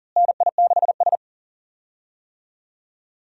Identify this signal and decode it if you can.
Morse code, heard as a keyed tone, about 40 wpm, 700 Hz. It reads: NI6S
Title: USA Call Signs - 4 Characters - Repeated 3x - 40wpm